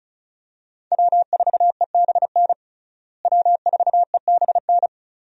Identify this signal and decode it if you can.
Morse code, heard as a keyed tone, about 34 wpm, 700 Hz. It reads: W4EBN W4EBN
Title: USA Call Signs - 5 Characters - Repeated 2x - 35wpm